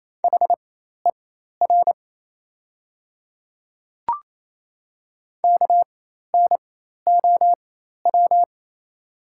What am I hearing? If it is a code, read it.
Morse code, heard as a keyed tone, about 28 wpm, spaced as for 13 wpm, 700 Hz. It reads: HER KNOW
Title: 4x Character Spacing - Top 300 Words - 28wpm